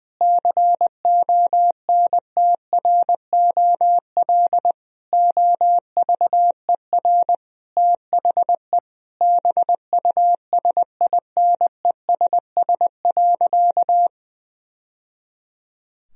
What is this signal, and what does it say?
Morse code, heard as a keyed tone, about 20 wpm, 700 Hz. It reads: CONTROL OVER THE BUSINESS.
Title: Sentences from Top 1000 Words - Review Part 3 - 20wpm